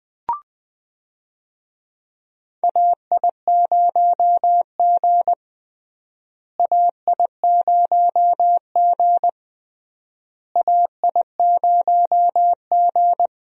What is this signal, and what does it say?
Morse code, heard as a keyed tone, about 20 wpm, 700 Hz. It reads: AI0G AI0G AI0G
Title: USA Call Signs - 4 Characters - Repeated 3x - 20wpm